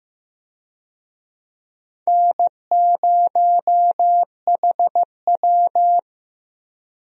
Morse code, heard as a keyed tone, about 15 wpm, 700 Hz. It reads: N0HW
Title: USA Call Signs - 4 Characters - Repeated 2x - 15wpm